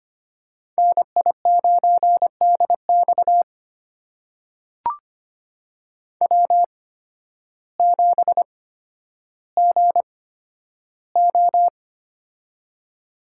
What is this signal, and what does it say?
Morse code, heard as a keyed tone, about 25 wpm, spaced as for 12 wpm, 700 Hz. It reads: NI9DX W7GO
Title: USA Call Signs - 8x Character Spacing - 25wpm